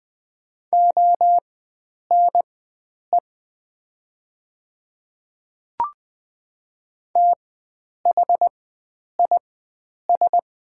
Morse code, heard as a keyed tone, about 20 wpm, spaced as for 9 wpm, 700 Hz. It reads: ONE THIS
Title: 4x Character Spacing - Top 300 Words - 20wpm